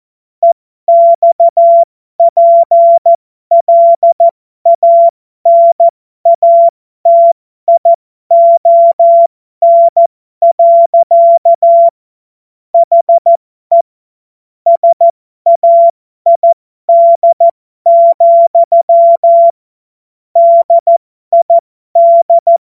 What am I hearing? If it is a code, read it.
Morse code, heard as a keyed tone, about 14 wpm, spaced as for 12 wpm, 680 Hz. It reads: EXPLANATION. HE SAID, DID
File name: Morsekurs / Witze